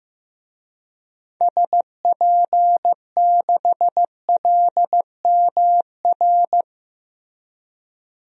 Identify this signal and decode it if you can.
Morse code, heard as a keyed tone, about 15 wpm, 700 Hz. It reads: SP6LMR
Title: DXCC-100 Call Signs - 6 Characters - Repeated 2x - 15wpm